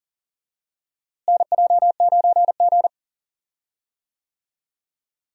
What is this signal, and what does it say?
Morse code, heard as a keyed tone, about 40 wpm, 700 Hz. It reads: NJ9G
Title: USA Call Signs - 4 Characters - Repeated 2x - 40wpm